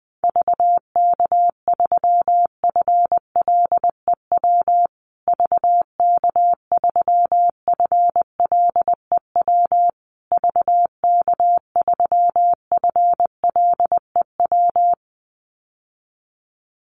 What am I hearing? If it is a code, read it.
Morse code, heard as a keyed tone, about 20 wpm, 700 Hz. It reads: VK3FLEW VK3FLEW VK3FLEW
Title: USA Call Signs Repeated 3x 20wpm